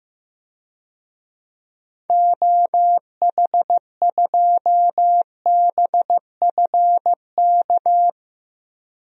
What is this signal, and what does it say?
Morse code, heard as a keyed tone, about 15 wpm, 700 Hz. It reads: OH2BFK
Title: DXCC-100 Call Signs - 6 Characters - 15wpm